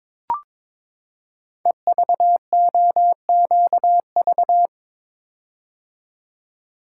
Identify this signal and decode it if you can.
Morse code, heard as a keyed tone, about 22 wpm, 700 Hz. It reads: EVOQV
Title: Code Groups - 5 Letters - 22wpm